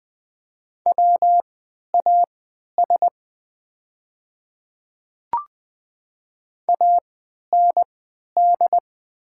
Morse code, heard as a keyed tone, about 20 wpm, spaced as for 11 wpm, 700 Hz. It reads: WAS AND
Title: 3x Character Spacing - Top 100 Words - 20wpm